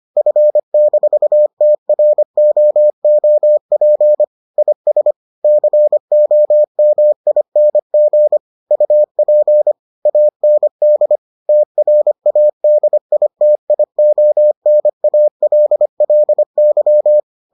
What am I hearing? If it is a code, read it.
Morse code, heard as a keyed tone, about 25 wpm, 600 Hz. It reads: F-TROOP IS COMING UP AND TRADITIONALLY